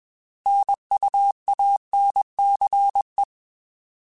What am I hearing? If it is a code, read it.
Morse code, heard as a keyed tone, about 21 wpm, 800 Hz. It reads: NUANCE
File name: Alphabetical / Alphabetical mp3s / Ns @20